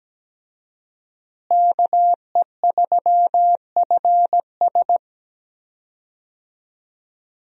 Morse code, heard as a keyed tone, about 17 wpm, 700 Hz. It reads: KE3FS
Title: USA Call Signs - 5 Characters - Repeated 2x - 17wpm